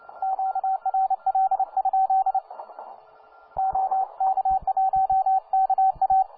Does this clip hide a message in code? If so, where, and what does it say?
Yes: Morse code from 0.2 to 6.2 s, heard as a keyed tone, about 29 wpm, 770 Hz, with radio noise. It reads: QRL? OK1KA